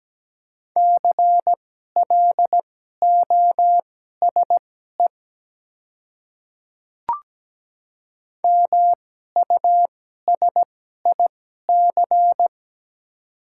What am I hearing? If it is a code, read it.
Morse code, heard as a keyed tone, about 17 wpm, spaced as for 12 wpm, 700 Hz. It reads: CLOSE MUSIC
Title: 2x Character Spacing - Top 300 Words - 17wpm